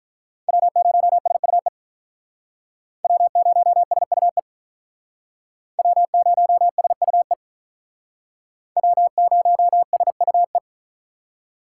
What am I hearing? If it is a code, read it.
Morse code, heard as a keyed tone, about 43 wpm, 700 Hz. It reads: W0SUE W0SUE W0SUE W0SUE
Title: Speed Racing - USA 1x3 Call Signs 35wpm